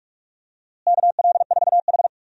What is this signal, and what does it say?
Morse code, heard as a keyed tone, about 47 wpm, 700 Hz. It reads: KL4H
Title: USA Call Signs - 4 Characters - Repeated 3x - 45wpm